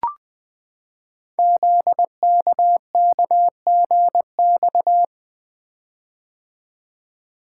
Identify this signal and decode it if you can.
Morse code, heard as a keyed tone, about 20 wpm, 700 Hz. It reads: ZKKGX
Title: Code Groups - 5 Letter:Numbers - 20wpm